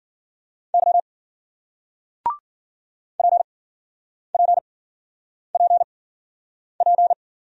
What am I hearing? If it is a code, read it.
Morse code, heard as a keyed tone, about 53 wpm, spaced as for 9 wpm, 700 Hz. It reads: Y PPPP